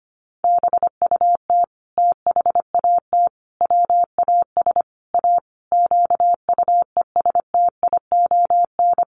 Morse code, heard as a keyed tone, about 25 wpm, 700 Hz. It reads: BUT THAT WAS A QUESTION
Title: Sentences from Top 1100 Words 25wpm